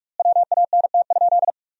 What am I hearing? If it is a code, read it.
Morse code, heard as a keyed tone, about 41 wpm, 700 Hz. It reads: WANT?
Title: 4x Word-Spacing - Sentences from Top 200 Words - 45wpm